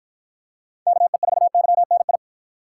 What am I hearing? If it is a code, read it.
Morse code, heard as a keyed tone, about 48 wpm, 700 Hz. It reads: KE4YNI